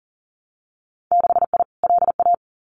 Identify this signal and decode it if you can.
Morse code, heard as a keyed tone, about 40 wpm, 700 Hz. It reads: 6I LA
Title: Winter Field Day Exchanges Encore 40wpm